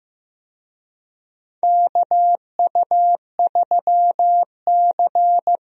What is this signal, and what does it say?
Morse code, heard as a keyed tone, about 15 wpm, 700 Hz. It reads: KU3C
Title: USA Call Signs - 4 Characters - Repeated 3x - 15wpm